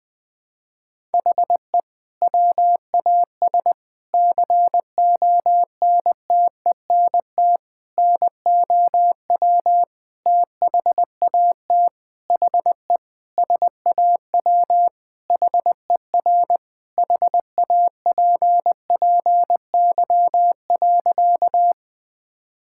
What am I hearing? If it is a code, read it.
Morse code, heard as a keyed tone, about 20 wpm, 700 Hz. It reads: HE WAS CONTENT NOW THAT HE SAW HER HAPPY.
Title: Sentences from Top 1000 Words - Review Part 6 - 20wpm